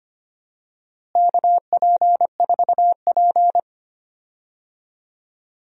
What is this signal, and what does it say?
Morse code, heard as a keyed tone, about 25 wpm, 700 Hz. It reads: KP4P
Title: DXCC-100 Call Signs - 4 Characters - Repeated 3x - 25wpm